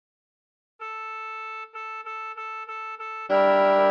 There is no code.